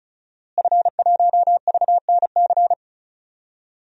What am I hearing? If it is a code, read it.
Morse code, heard as a keyed tone, about 36 wpm, 700 Hz. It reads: F1VNC
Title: DXCC-100 Call Signs - 5 Characters - Repeated 3x - 35wpm